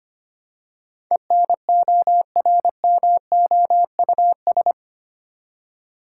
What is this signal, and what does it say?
Morse code, heard as a keyed tone, about 25 wpm, 700 Hz. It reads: ENORMOUS
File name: Eight Letter Words - 25wpm